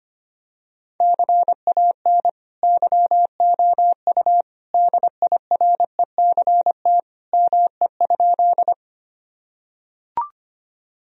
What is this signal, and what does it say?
Morse code, heard as a keyed tone, about 25 wpm, 700 Hz. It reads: CAN YOU DIRECT ME?
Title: Sentences from Top 1000 Words - Review Part 2 - 25wpm